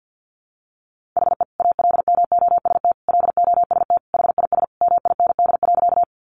Morse code, heard as a keyed tone, about 50 wpm, 700 Hz. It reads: HE ALMOST LOST HIS MIND.
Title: Sentences from Top 1400 Words 50wpm